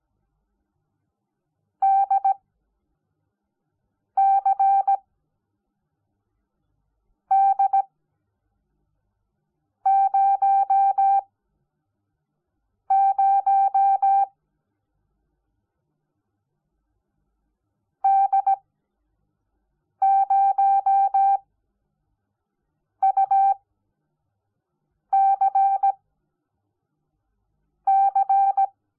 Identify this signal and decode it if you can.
Morse code, heard as a keyed tone, about 17 wpm, spaced as for 4 wpm, 780 Hz. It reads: DCD00 D0UCC
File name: Onlinelehrgang / morsen / DJ4UF Morsen05